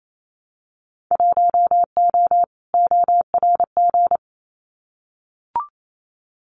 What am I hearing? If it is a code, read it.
Morse code, heard as a keyed tone, about 28 wpm, 700 Hz. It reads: 1O ORG
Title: Winter Field Day Exchanges 28wpm